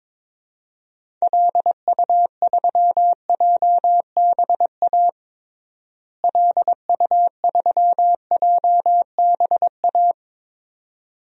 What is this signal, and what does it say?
Morse code, heard as a keyed tone, about 21 wpm, 700 Hz. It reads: LU3JBA LU3JBA
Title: DXCC-100 Call Signs - 6 Characters - Repeated 2x - 22wpm